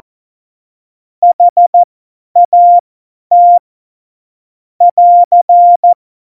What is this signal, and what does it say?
Morse code, heard as a keyed tone, about 14 wpm, spaced as for 10 wpm, 700 Hz. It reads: HAT <AR>
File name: Morsekurs / Witze / Witz 12. November 2019, 50 F70